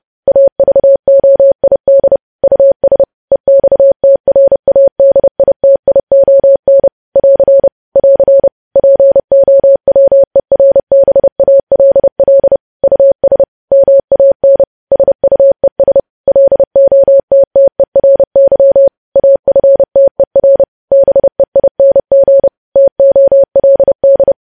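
Morse code, heard as a keyed tone, about 30 wpm, 550 Hz. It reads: AVOID US EXTRADITION <AR> <AR> POWERBALL US MAN SUES LOTTERY AFTER BEING TOLD
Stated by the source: Morse code news headlines sent at 30 WPM.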